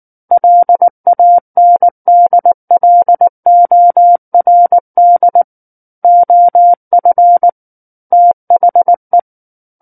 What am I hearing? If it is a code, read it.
Morse code, heard as a keyed tone, about 19 wpm, 700 Hz. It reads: LANDLORD OF THE
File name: Book One / 10-Chapter 10 - 19 WPM